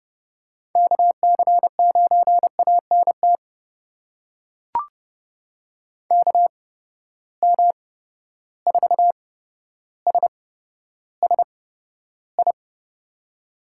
Morse code, heard as a keyed tone, about 30 wpm, spaced as for 13 wpm, 700 Hz. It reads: KC9ANT KM4SSI
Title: USA Call Signs - 6 Characters - 8x Character Spacing - 30wpm